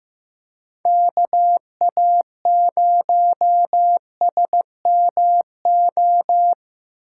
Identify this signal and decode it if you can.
Morse code, heard as a keyed tone, about 15 wpm, 700 Hz. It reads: KA0SMO
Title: USA Call Signs - 6 Characters - 2x Character Spacing - 15wpm